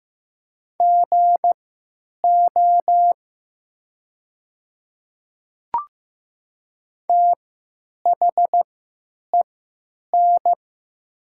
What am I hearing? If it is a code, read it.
Morse code, heard as a keyed tone, about 15 wpm, spaced as for 9 wpm, 690 Hz. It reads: GO THEN
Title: 3x Character Spacing - Top 100 Words - 15wpm